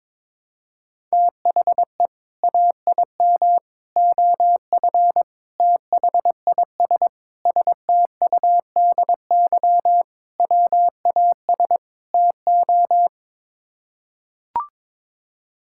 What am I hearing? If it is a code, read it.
Morse code, heard as a keyed tone, about 22 wpm, 700 Hz. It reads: THE AIM OF THIS STUDY WAS TO
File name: practice / 22wpm / Sets of 7 Words - 22wpm